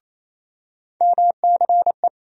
Morse code, heard as a keyed tone, about 28 wpm, 700 Hz. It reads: MCE
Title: USA Call Sign Postfixes - 3 Letters 28wpm